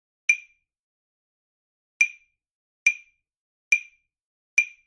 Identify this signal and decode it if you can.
There is no code.